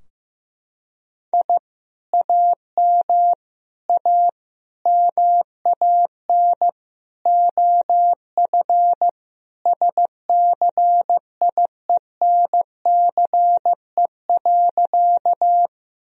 Morse code, heard as a keyed tone, about 16 wpm, 700 Hz. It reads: I AM A MAN OF SCIENCE.
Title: Sentences from Top 1000 Words - Review Part 2 - 15wpm